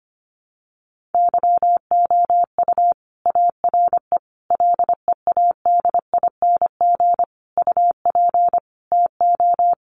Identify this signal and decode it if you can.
Morse code, heard as a keyed tone, about 25 wpm, 700 Hz. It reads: YOU ARE LEADING UP TO
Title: Sentences from Top 1100 Words 25wpm